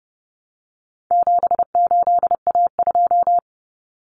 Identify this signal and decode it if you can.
Morse code, heard as a keyed tone, about 30 wpm, 700 Hz. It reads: 78A2
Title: Common Cut Numbers - 4-Digit 30wpm